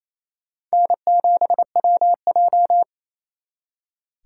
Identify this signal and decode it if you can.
Morse code, heard as a keyed tone, about 35 wpm, 700 Hz. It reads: N7WJ
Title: Speed Racing - USA Call Signs 28wpm